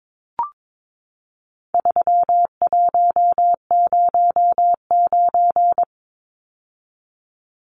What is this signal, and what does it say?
Morse code, heard as a keyed tone, about 22 wpm, 700 Hz. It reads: 3109
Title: Numbers - 4-Digit 22wpm